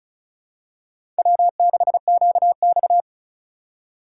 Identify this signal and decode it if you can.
Morse code, heard as a keyed tone, about 36 wpm, 700 Hz. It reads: W6QX
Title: USA Call Signs - 4 Characters - 2x Character Spacing - 35wpm